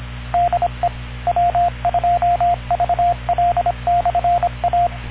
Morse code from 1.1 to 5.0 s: W2VL/A